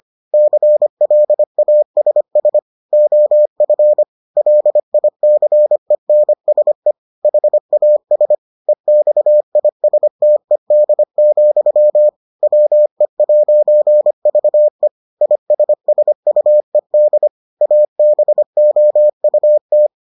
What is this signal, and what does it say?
Morse code, heard as a keyed tone, about 25 wpm, 600 Hz. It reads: CLASS OF LICENSE HAS EXISTED, WE'VE ISSUED ABOUT